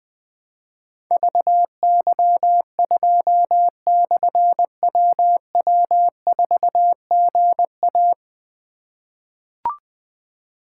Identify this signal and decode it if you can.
Morse code, heard as a keyed tone, about 20 wpm, 700 Hz. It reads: VY2/WW4GA
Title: Difficult Call Signs 20wpm